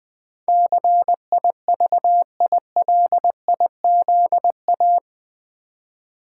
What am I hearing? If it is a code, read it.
Morse code, heard as a keyed tone, about 20 wpm, 700 Hz. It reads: CIVILIZA
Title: Build A Word - 4 Letters At A Time - 12-Letter Words - 20wpm